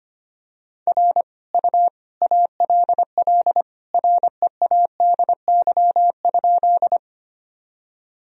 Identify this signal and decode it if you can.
Morse code, heard as a keyed tone, about 25 wpm, 700 Hz. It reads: R U ALL READY?
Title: Sentences from Top 400 Words and Abbreviations 25wpm